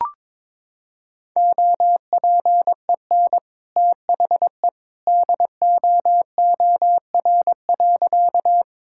Morse code, heard as a keyed tone, about 22 wpm, 700 Hz. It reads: OPEN THE DOOR.
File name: practice / 22wpm / Sentences with 2-Word Phrases - 22wpm